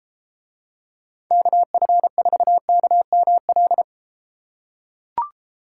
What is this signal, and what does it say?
Morse code, heard as a keyed tone, about 32 wpm, 700 Hz. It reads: KF4KML